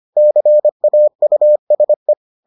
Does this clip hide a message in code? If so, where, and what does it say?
Yes: Morse code from 0.2 to 2.1 s, heard as a keyed tone, about 25 wpm, 600 Hz. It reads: CAUSE